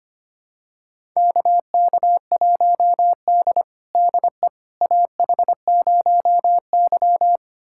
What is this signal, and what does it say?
Morse code, heard as a keyed tone, about 25 wpm, 700 Hz. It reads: KK1D DE AH0Y
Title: USA Callsign DE Callsign - 4 Characters - 25wpm